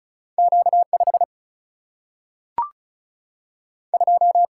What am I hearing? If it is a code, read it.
Morse code, heard as a keyed tone, about 36 wpm, 700 Hz. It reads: Q5 2
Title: Single-Letter Single-Number 35wpm